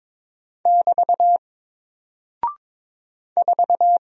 Morse code, heard as a keyed tone, about 22 wpm, 700 Hz. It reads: <BT> 4
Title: Single Character (Letter:Number:Punctuation:Prosign) Alt 22wpm